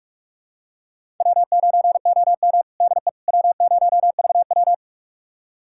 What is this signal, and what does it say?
Morse code, heard as a keyed tone, about 46 wpm, 700 Hz. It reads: W9OM DE W0VW
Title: USA Callsign DE Callsign - 4 Characters - 45wpm